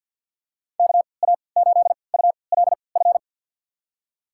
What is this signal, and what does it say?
Morse code, heard as a keyed tone, about 53 wpm, spaced as for 29 wpm, 700 Hz. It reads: KA8ULF
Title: USA Call Signs - 6 Characters - 3x Character Spacing - 50wpm